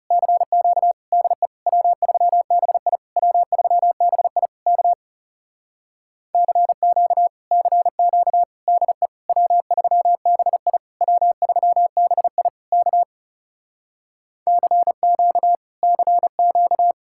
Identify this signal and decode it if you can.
Morse code, heard as a keyed tone, about 35 wpm, 700 Hz. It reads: CQ DE W3BI W3BI K CQ CQ DE W3BI W3BI K CQ CQ
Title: Speed-Racing Common QSO Phrases Encore - 30wpm